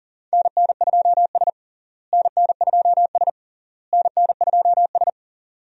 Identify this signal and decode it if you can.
Morse code, heard as a keyed tone, about 40 wpm, 700 Hz. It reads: NN2S NN2S NN2S
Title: USA Call Signs - 4 Characters - Repeated 3x - 40wpm